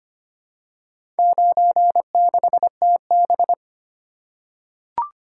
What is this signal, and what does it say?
Morse code, heard as a keyed tone, about 25 wpm, 700 Hz. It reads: 96TB
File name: practice / 25wpm / Four Letter-Number Combinations - 25wpm